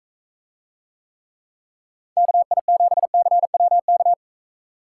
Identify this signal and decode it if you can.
Morse code, heard as a keyed tone, about 43 wpm, 700 Hz. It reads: KI7CWK